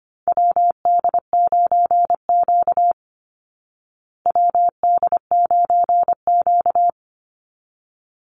Morse code, heard as a keyed tone, about 25 wpm, 700 Hz. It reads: WD9Q WD9Q
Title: Call Signs Repeated 3x - 25wpm